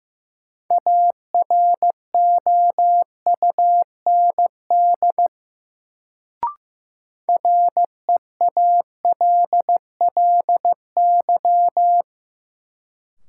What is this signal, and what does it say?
Morse code, heard as a keyed tone, about 15 wpm, 700 Hz. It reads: AROUND REALLY
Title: Sentences from Top 600 Words - New Word Drill - 15wpm